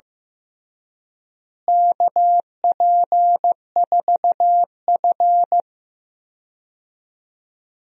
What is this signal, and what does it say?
Morse code, heard as a keyed tone, about 15 wpm, 700 Hz. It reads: KP4F
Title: DXCC-100 Call Signs - 4 Characters - Repeated 2x - 15wpm